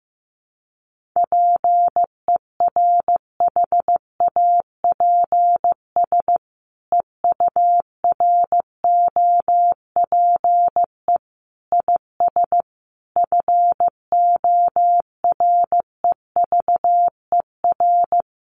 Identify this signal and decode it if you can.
Morse code, heard as a keyed tone, about 15 wpm, 700 Hz. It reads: PERHAPS EUROPE IS FOREVER
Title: A Sound Of Thunder 15wpm